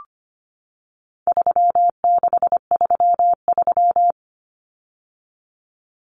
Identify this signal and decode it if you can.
Morse code, heard as a keyed tone, about 25 wpm, 700 Hz. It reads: 3633
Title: Numbers - 4-Digit 25wpm